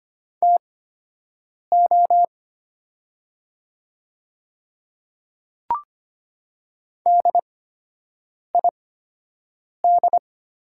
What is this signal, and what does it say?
Morse code, heard as a keyed tone, about 25 wpm, spaced as for 7 wpm, 700 Hz. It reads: TO DID
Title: 8x Character Spacing - Top 100 Words - 25wpm